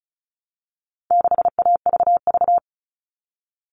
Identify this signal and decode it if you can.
Morse code, heard as a keyed tone, about 34 wpm, 700 Hz. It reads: 6AVV
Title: All Cut Numbers - 4-Digit 35wpm